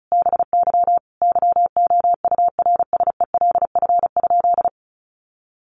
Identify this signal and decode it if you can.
Morse code, heard as a keyed tone, about 34 wpm, 700 Hz. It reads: BY YOURSELF?
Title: Sentences from Top 1100 Words 35wpm